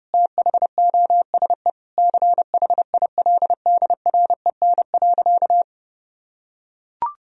Morse code, heard as a keyed tone, about 32 wpm, 700 Hz. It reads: THOSE CHILDREN.
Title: Sentences from Top 1000 Words - Review Part 1 - 30wpm